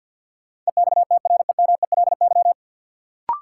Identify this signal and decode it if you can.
Morse code, heard as a keyed tone, about 48 wpm, 700 Hz. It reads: EXTREMELY
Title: Sentences from Top 1200 Words - New Word Drill Alt - 50wpm